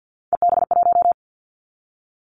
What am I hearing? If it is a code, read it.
Morse code, heard as a keyed tone, about 50 wpm, 700 Hz. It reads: EB1
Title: Callsign Prefixes for Top 25 DXCC - 50wpm